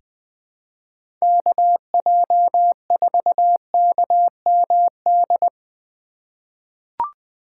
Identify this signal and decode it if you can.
Morse code, heard as a keyed tone, about 20 wpm, 700 Hz. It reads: KJ4KMD
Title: USA Call Signs - 6 Characters - 3x Character Spacing - 20wpm